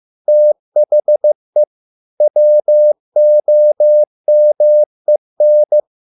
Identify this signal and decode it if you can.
Morse code, heard as a keyed tone, about 15 wpm, 600 Hz. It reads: THE WOMEN